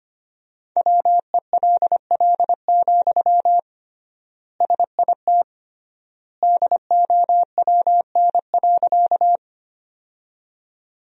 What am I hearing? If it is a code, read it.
Morse code, heard as a keyed tone, about 25 wpm, 700 Hz. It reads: WELL, SIT DOWN.
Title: 3x Word-Spacing - Sentences from Top 500 Words - 25wpm